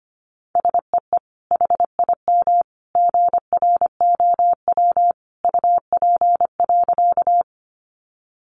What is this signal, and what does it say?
Morse code, heard as a keyed tone, about 25 wpm, 700 Hz. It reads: SEE HIM GROW UP.
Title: Sentences from Top 300 Words 25wpm